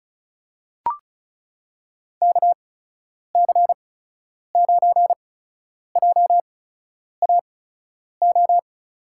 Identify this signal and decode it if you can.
Morse code, heard as a keyed tone, about 36 wpm, spaced as for 10 wpm, 700 Hz. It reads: KC9JAO
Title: USA Call Signs - 6 Characters - 8x Character Spacing - 35wpm